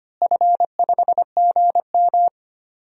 Morse code, heard as a keyed tone, about 25 wpm, 700 Hz. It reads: F5GM
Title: DXCC-100 Call Signs - 4 Characters - Repeated 2x - 25wpm